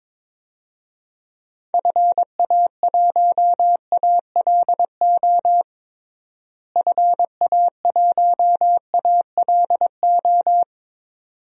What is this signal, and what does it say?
Morse code, heard as a keyed tone, about 21 wpm, 700 Hz. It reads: FA1ALO FA1ALO
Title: DXCC-100 Call Signs - 6 Characters - Repeated 2x - 22wpm